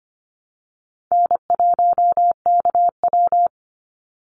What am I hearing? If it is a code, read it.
Morse code, heard as a keyed tone, about 25 wpm, 700 Hz. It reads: N1KW
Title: Easy Call Signs Repeated 2x - 25wpm